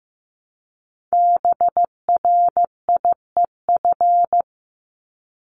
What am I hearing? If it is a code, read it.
Morse code, heard as a keyed tone, about 15 wpm, 700 Hz. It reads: BRIEF
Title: LY Suffix - Top 100 Words with Base Words 15wpm